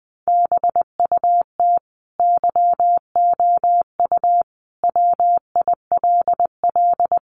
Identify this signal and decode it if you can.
Morse code, heard as a keyed tone, about 20 wpm, 700 Hz. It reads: BUT YOU WILL
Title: Sentences from Top 1500 Words 20wpm